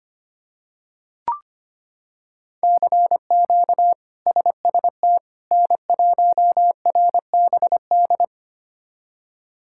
Morse code, heard as a keyed tone, about 25 wpm, 700 Hz. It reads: CQ SST N1RBD
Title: K1USN SST QSO Practice - 25wpm